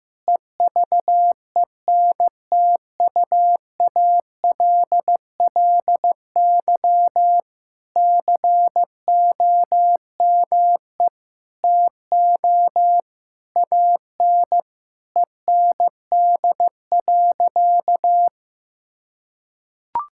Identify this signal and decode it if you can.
Morse code, heard as a keyed tone, about 15 wpm, 700 Hz. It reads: EVENTUALLY COME TO AN END.
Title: Sentences from Top 1000 Words - Review Part 6 - 15wpm